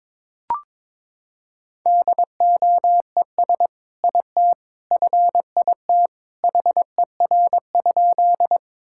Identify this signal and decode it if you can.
Morse code, heard as a keyed tone, about 22 wpm, 700 Hz. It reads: DOES IT FIT HER?
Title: Sentences from Top 1000 Words - Review Part 5 - 22wpm